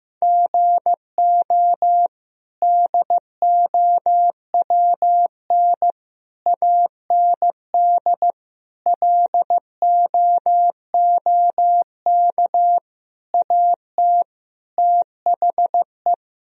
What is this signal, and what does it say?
Morse code, heard as a keyed tone, about 15 wpm, 700 Hz. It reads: GO DOWN AND LOOK AT THE
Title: Sentences from Top 1000 Words - Review Part 1 - 15wpm